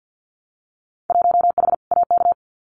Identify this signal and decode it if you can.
Morse code, heard as a keyed tone, about 50 wpm, 700 Hz. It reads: 1H AK
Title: Winter Field Day Exchanges Encore 50wpm